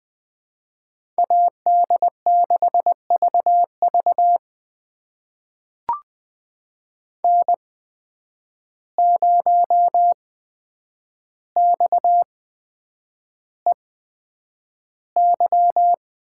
Morse code, heard as a keyed tone, about 20 wpm, spaced as for 9 wpm, 700 Hz. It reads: AD6VV N0XEY
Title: USA Call Signs - 5 Characters - 8x Character Spacing - 20wpm